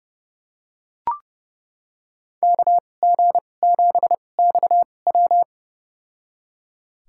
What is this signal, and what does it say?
Morse code, heard as a keyed tone, about 32 wpm, spaced as for 22 wpm, 700 Hz. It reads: KG7XW
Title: USA Call Signs - 5 Characters - 2x Character Spacing - 30wpm